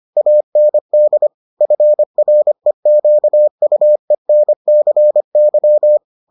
Morse code, heard as a keyed tone, about 25 wpm, 600 Hz. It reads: AND FREQUENCY